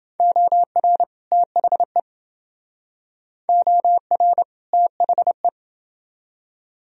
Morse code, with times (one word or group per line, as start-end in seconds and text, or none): 0.2-1.0 OR
1.3-2.0 THE
3.5-4.4 OR
4.7-5.5 THE